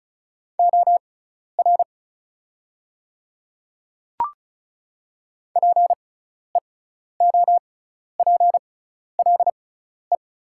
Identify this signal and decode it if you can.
Morse code, heard as a keyed tone, about 34 wpm, spaced as for 12 wpm, 700 Hz. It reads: OR PEOPLE
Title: 6x Character Spacing - Top 100 Words - 35wpm